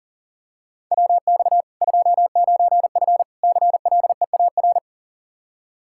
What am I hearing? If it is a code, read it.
Morse code, heard as a keyed tone, about 40 wpm, 700 Hz. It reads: WX 29F CLEAR
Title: Practice QSO - Exchange 2 - Info - 40wpm